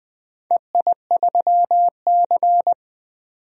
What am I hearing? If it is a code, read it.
Morse code, heard as a keyed tone, about 20 wpm, 700 Hz. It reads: EI3C
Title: DXCC-100 Call Signs - 4 Characters - 20wpm